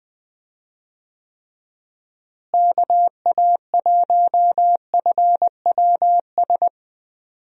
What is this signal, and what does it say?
Morse code, heard as a keyed tone, about 20 wpm, 700 Hz. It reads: KA1FWS
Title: USA Call Signs - 6 Characters - Repeated 2x - 20wpm